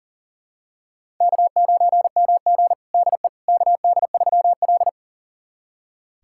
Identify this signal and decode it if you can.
Morse code, heard as a keyed tone, about 40 wpm, 700 Hz. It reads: K9MG DE KD3L
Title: USA Callsign DE Callsign - 4 Characters - 40wpm